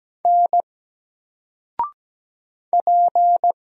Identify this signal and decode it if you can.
Morse code, heard as a keyed tone, about 17 wpm, 700 Hz. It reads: N P
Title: Single Letter-Number 17wpm